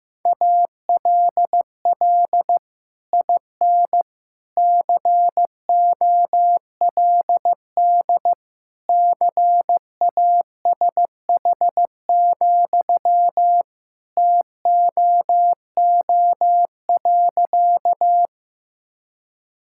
Morse code, heard as a keyed tone, about 16 wpm, 700 Hz. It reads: ALL IN COLD CASH, TOO.
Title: Sentences from Top 1000 Words - Review Part 5 - 15wpm